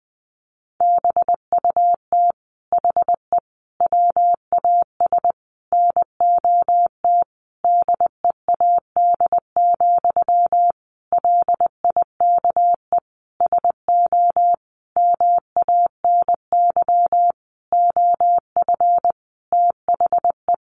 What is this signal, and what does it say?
Morse code, heard as a keyed tone, about 19 wpm, 700 Hz. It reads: BUT HE WAS NOT DEAD, LIKE SO MANY OF THE
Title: Sentences from Top 1500 Words 20wpm